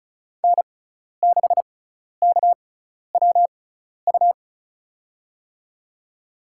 Morse code, heard as a keyed tone, about 36 wpm, spaced as for 12 wpm, 700 Hz. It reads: N6KWU
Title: USA Call Signs - 5 Characters - 6x Character Spacing - 35wpm